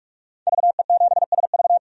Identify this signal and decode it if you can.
Morse code, heard as a keyed tone, about 44 wpm, 690 Hz. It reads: VE7SV
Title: DXCC-100 Call Signs - 5 Characters - 45wpm